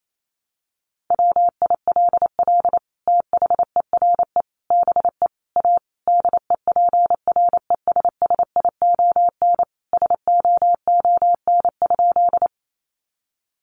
Morse code, heard as a keyed tone, about 28 wpm, 700 Hz. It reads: WILL THERE BE A DEPRESSION SOON?
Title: Sentences from Top 1300 Words 28wpm